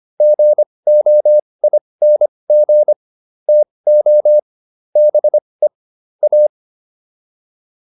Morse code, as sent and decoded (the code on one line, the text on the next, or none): --. --- .. -. --. / - --- / -... / .-
GOING TO BE A